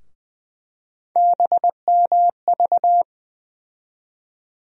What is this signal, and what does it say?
Morse code, heard as a keyed tone, about 21 wpm, 700 Hz. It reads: BMV
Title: USA Call Sign Postfixes - 3 Letters 20wpm